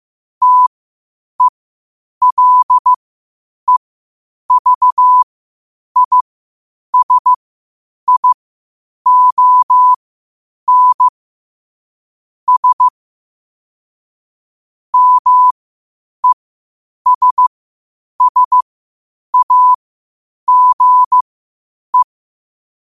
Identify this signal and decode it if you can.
Morse code, heard as a keyed tone, about 15 wpm, spaced as for 8 wpm, 1000 Hz. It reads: TELEVISION S MESSAGE
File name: qotdfiles / 8wpm